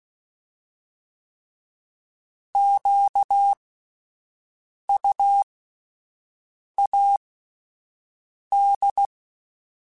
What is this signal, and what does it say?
Morse code, heard as a keyed tone, about 16 wpm, spaced as for 5 wpm, 800 Hz. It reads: QUAD
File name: fromSMs / FromSMs mp3s / Antennas@5